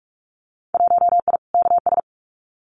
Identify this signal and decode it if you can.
Morse code, heard as a keyed tone, about 46 wpm, 700 Hz. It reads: 1I KS
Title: Winter Field Day Exchanges Encore 45wpm